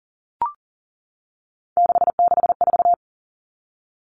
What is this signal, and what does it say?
Morse code, heard as a keyed tone, about 40 wpm, 700 Hz. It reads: 664
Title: Common Cut Numbers - 3-Digit 40wpm